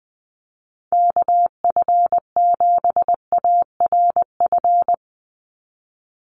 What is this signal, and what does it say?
Morse code, heard as a keyed tone, about 20 wpm, 700 Hz. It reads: KF7ARF
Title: USA Call Signs 20wpm